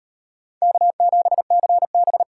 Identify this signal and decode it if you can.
Morse code, heard as a keyed tone, about 38 wpm, 700 Hz. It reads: K7CB